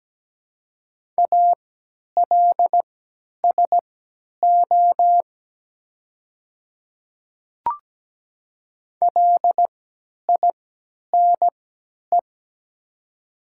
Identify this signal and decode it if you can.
Morse code, heard as a keyed tone, about 17 wpm, spaced as for 10 wpm, 700 Hz. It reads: ALSO LINE
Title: 3x Character Spacing - Top 200 Words - 17wpm